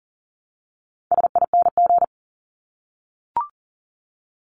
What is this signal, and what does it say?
Morse code, heard as a keyed tone, about 40 wpm, 700 Hz. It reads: SING
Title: Top 500 Words 40wpm